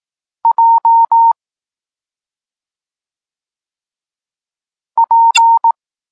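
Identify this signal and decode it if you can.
Morse code, heard as a keyed tone, about 18 wpm, 920 Hz. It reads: J P
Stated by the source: Morse code lesson for learning the letters K and X, from the SP9MOA Niepołomicki Klub Krótkofalowców course.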